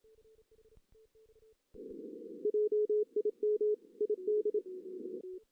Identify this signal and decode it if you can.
Morse code, heard as a keyed tone, about 26 wpm, 420 Hz. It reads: JIM ID
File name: 2016 JAN CW NAQP / 2016 Jan naqp k5ot - 0-4